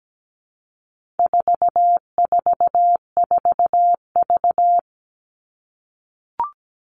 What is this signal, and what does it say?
Morse code, heard as a keyed tone, about 17 wpm, 700 Hz. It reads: 444V
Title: All Cut Numbers - 4-Digit 17wpm